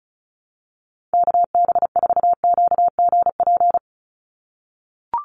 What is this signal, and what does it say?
Morse code, heard as a keyed tone, about 34 wpm, 700 Hz. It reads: KB4QGP
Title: USA Call Signs 35wpm